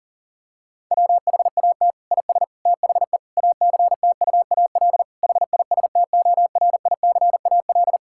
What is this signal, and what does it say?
Morse code, heard as a keyed tone, about 40 wpm, 700 Hz. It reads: WHAT IS THE ACTUAL HISTORICAL
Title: Sentences from Top 1000 Words - Review Part 5 - 40wpm